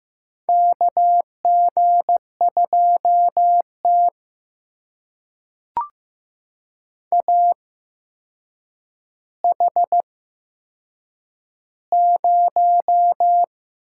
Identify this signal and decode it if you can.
Morse code, heard as a keyed tone, about 15 wpm, spaced as for 7 wpm, 700 Hz. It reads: KG2T AH0
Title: USA Call Signs - 4 Characters - 8x Character Spacing - 15wpm